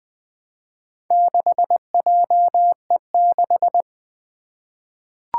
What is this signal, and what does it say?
Morse code, heard as a keyed tone, about 20 wpm, 700 Hz. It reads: 6JE6
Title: Four Challenging Letter-Number Combinations 20wpm